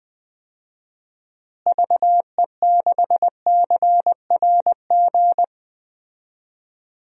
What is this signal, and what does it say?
Morse code, heard as a keyed tone, about 21 wpm, 700 Hz. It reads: VE6CRG
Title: DXCC-100 Call Signs - 6 Characters - Repeated 3x - 20wpm